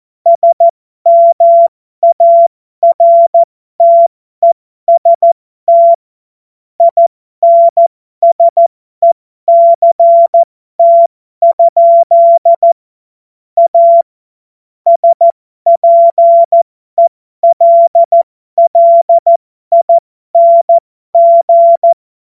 Morse code, heard as a keyed tone, about 14 wpm, spaced as for 12 wpm, 680 Hz. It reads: SMARTEST INSECT? A SPELLING
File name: Morsekurs / Witze